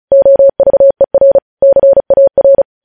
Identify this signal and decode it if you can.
Morse code, heard as a keyed tone, about 36 wpm, 550 Hz. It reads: OVER CAR
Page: News Headlines in Morse Code at 35 WPM